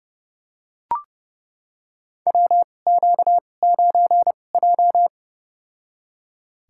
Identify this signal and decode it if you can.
Morse code, heard as a keyed tone, about 32 wpm, spaced as for 22 wpm, 700 Hz. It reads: WQ9J
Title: USA Call Signs - 4 Characters - 2x Character Spacing - 30wpm